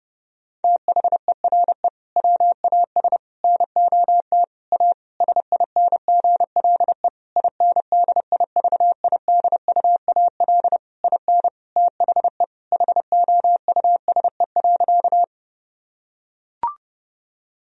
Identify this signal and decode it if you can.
Morse code, heard as a keyed tone, about 30 wpm, 700 Hz. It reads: THERE WAS NOT A SINGLE INDIVIDUAL IN THE HOUSE.
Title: Sentences from Top 1000 Words - Review Part 4 - 30wpm